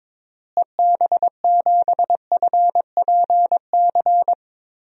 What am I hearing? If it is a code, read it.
Morse code, heard as a keyed tone, about 22 wpm, 700 Hz. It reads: EB7FPC
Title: DXCC-100 Call Signs - 6 Characters - Repeated 3x - 22wpm